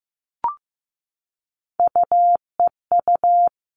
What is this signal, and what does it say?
Morse code, heard as a keyed tone, about 15 wpm, 700 Hz. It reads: UEU
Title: All Cut Numbers - 3-Digit 15wpm